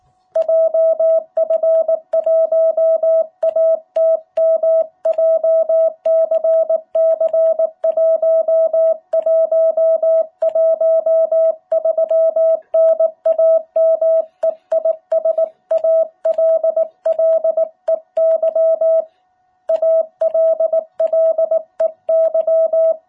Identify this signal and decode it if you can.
Morse code, heard as a keyed tone, about 19 wpm, 660 Hz. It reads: JF1ATM JCC1113 NAME IS ALLEY ALLEY
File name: jh8cbh / 2021cw